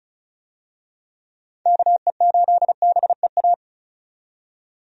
Morse code, heard as a keyed tone, about 34 wpm, 700 Hz. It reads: KE8BEA
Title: USA Call Signs - 6 Characters - 35wpm